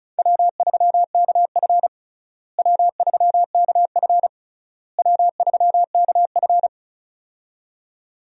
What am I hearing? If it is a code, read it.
Morse code, heard as a keyed tone, about 34 wpm, 700 Hz. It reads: W3KF W3KF W3KF
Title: USA Call Signs - 4 Characters - Repeated 3x - 35wpm